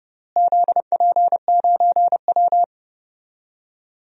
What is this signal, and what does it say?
Morse code, heard as a keyed tone, about 31 wpm, 700 Hz. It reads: ZP9W
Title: DXCC-100 Call Signs - 4 Characters - Repeated 3x - 30wpm